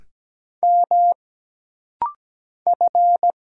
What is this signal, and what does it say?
Morse code, heard as a keyed tone, about 17 wpm, 700 Hz. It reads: M F
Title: Single Letter-Number Alt - Rapid-Fire 17wpm